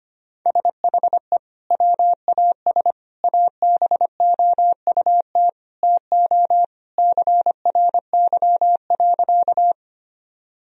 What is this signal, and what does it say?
Morse code, heard as a keyed tone, about 25 wpm, 700 Hz. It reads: SHE WAS ABOUT TO CRY.
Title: Sentences from Top 1000 Words - Review Part 2 - 25wpm